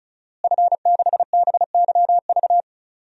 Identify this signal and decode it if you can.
Morse code, heard as a keyed tone, about 34 wpm, 700 Hz. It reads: F6BYV
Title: DXCC-100 Call Signs - 5 Characters - Repeated 3x - 35wpm